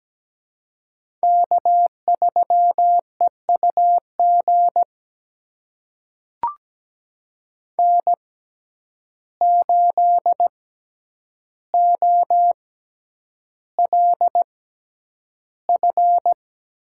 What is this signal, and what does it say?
Morse code, heard as a keyed tone, about 17 wpm, spaced as for 9 wpm, 700 Hz. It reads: K3EUG N8OLF
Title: USA Call Signs - 5 Characters - 6x Character Spacing - 17wpm